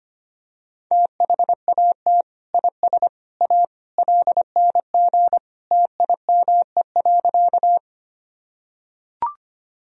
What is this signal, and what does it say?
Morse code, heard as a keyed tone, about 25 wpm, 700 Hz. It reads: THAT IS A LNG TIME.
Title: Sentences from Top 100 Words and Abbreviations 25wpm